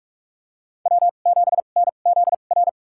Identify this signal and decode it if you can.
Morse code, heard as a keyed tone, about 44 wpm, spaced as for 32 wpm, 690 Hz. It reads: W7NZR
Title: USA Call Signs - 5 Characters - 2x Character Spacing - 45wpm